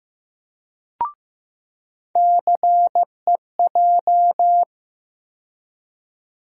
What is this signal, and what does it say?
Morse code, heard as a keyed tone, about 15 wpm, 700 Hz. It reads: CEJ
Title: USA Call Sign Postfixes - 3 Letters 15wpm